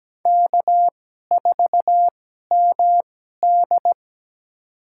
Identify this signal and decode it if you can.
Morse code, heard as a keyed tone, about 17 wpm, spaced as for 12 wpm, 700 Hz. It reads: K4MD
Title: USA Call Signs - 4 Characters - 2x Character Spacing - 17wpm